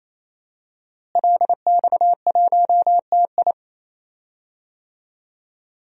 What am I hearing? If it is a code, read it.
Morse code, heard as a keyed tone, about 28 wpm, 700 Hz. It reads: LX1TI
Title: DXCC-100 Call Signs - 5 Characters - Repeated 3x - 28wpm